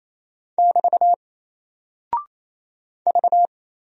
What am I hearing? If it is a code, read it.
Morse code, heard as a keyed tone, about 27 wpm, 700 Hz. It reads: <BT> V
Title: Single Character (Letter:Number:Punctuation:Prosign) 28wpm